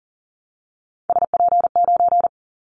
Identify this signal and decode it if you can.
Morse code, heard as a keyed tone, about 40 wpm, 700 Hz. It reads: SP9
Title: Callsign Prefixes for Top 25 DXCC - Repeated 3x - 40wpm